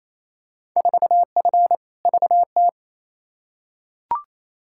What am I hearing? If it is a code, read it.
Morse code, heard as a keyed tone, about 28 wpm, 700 Hz. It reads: 4F VT
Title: Field Day Exchanges Encore - 28wpm